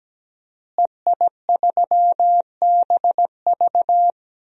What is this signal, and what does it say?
Morse code, heard as a keyed tone, about 17 wpm, 700 Hz. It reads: EI3BV
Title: DXCC-100 Call Signs - 5 Characters - Repeated 3x - 17wpm